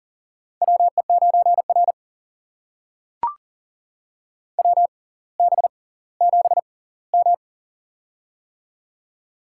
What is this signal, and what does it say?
Morse code, heard as a keyed tone, about 40 wpm, spaced as for 20 wpm, 700 Hz. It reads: WE9R WB7M
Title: USA Call Signs - 4 Characters - 6x Character Spacing - 40wpm